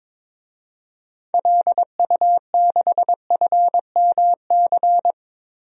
Morse code, heard as a keyed tone, about 22 wpm, 700 Hz. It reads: LU6FMC
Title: DXCC-100 Call Signs - 6 Characters - 22wpm